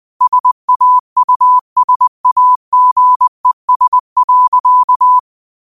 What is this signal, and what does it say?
Morse code, heard as a keyed tone, about 20 wpm, 1000 Hz. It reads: SAUSAGES.
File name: qotdfiles / 20wpm